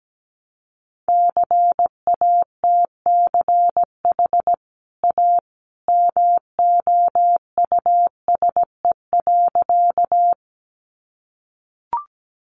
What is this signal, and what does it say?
Morse code, heard as a keyed tone, about 17 wpm, 700 Hz. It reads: CATCH A MOUSE.
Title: Sentences from Top 1500 Words 17wpm